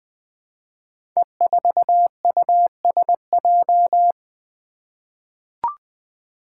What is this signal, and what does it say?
Morse code, heard as a keyed tone, about 20 wpm, 700 Hz. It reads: E4USJ
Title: Five Challenging Letter-Number Combinations 20wpm